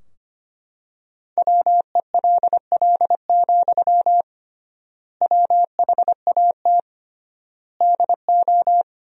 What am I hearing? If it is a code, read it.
Morse code, heard as a keyed tone, about 25 wpm, 700 Hz. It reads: WELL, WHAT DO
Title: 3x Word-Spacing - Sentences from Top 200 Words - 25wpm